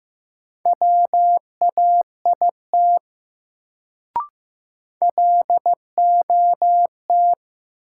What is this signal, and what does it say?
Morse code, heard as a keyed tone, about 15 wpm, 700 Hz. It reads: WAIT LOT
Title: Sentences from Top 500 Words - New Word Drill - 15wpm